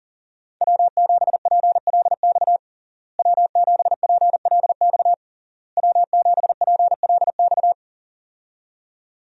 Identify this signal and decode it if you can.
Morse code, heard as a keyed tone, about 37 wpm, 700 Hz. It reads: W7PLX W7PLX W7PLX
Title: USA Call Signs - 5 Characters - Repeated 3x - 40wpm